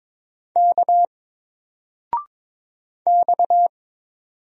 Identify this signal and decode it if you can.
Morse code, heard as a keyed tone, about 23 wpm, 700 Hz. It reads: K X
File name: Single-Letter-Number-22wpm